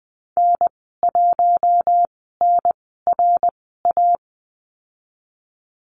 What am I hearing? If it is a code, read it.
Morse code, heard as a keyed tone, about 20 wpm, spaced as for 15 wpm, 700 Hz. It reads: N1NRA
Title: 055 Lesson 10g - Letter r - Call Signs 20wpm